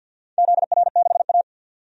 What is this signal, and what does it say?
Morse code, heard as a keyed tone, about 47 wpm, 700 Hz. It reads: ZR6A
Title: DXCC-100 Call Signs - 4 Characters - 50wpm